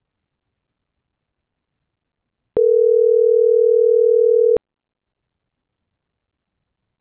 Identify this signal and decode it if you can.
Ringback tone (North America), 440 + 480 Hz.